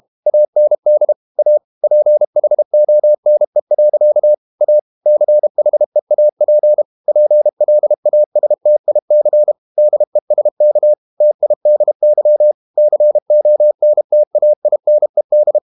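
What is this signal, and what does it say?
Morse code, heard as a keyed tone, about 32 wpm, 600 Hz. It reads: AND A PHONE. A CHEAP PLASTIC DESK TIDY CONTAINED
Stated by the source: Morse code practice at 32 wpm from CW med Gnister, reading Ken Follett's novel Never in English.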